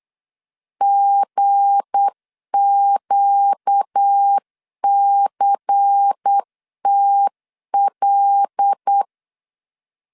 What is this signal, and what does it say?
Morse code, heard as a keyed tone, about 8 wpm, 800 Hz. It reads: GQCTL
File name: LEZIONI / 7 GF9Z / 7 GF 8wpm 3108